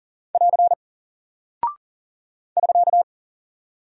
Morse code, heard as a keyed tone, about 40 wpm, 700 Hz. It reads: <AR> <SK>
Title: Single Character (Letter:Number:Punctuation:Prosign) Alt 40wpm